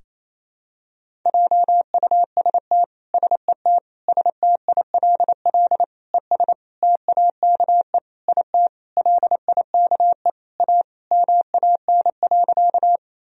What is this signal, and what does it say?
Morse code, heard as a keyed tone, about 27 wpm, 700 Hz. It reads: JUST SET STILL ES TAKE IT LIKE A MAN.
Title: Sentences from Top 300 Words and Abbreviations 28wpm